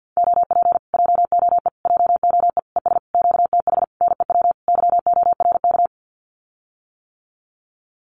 Morse code, heard as a keyed tone, about 50 wpm, 700 Hz. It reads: OP JOE JOE ES QTH NEW YORK
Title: Common QSO Phrases Encore 50wpm